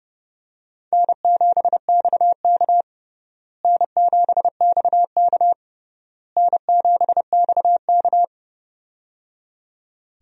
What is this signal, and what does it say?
Morse code, heard as a keyed tone, about 30 wpm, 700 Hz. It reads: N7XK N7XK N7XK
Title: USA Call Signs - 4 Characters - Repeated 3x - 30wpm